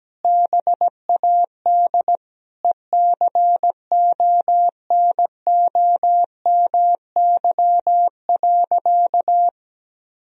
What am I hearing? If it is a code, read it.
Morse code, heard as a keyed tone, about 17 wpm, 700 Hz. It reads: BAD ECONOMY.
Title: Sentences from Top 1000 Words - Review Part 3 - 17wpm